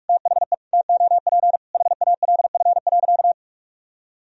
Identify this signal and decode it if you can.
Morse code, heard as a keyed tone, about 41 wpm, 700 Hz. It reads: THE TOP HALF.
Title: Sentences from Top 1000 Words - Review Part 2 - 45wpm